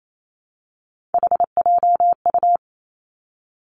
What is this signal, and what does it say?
Morse code, heard as a keyed tone, about 28 wpm, 700 Hz. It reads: HJU
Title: Three Challenging Letter-Number Combinations 28wpm